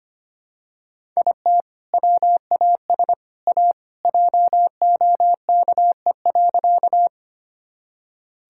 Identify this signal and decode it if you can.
Morse code, heard as a keyed tone, about 25 wpm, 700 Hz. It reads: IT WAS A JOKE.
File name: practice / 25wpm / Sentences with 3-Word Phrases - 25wpm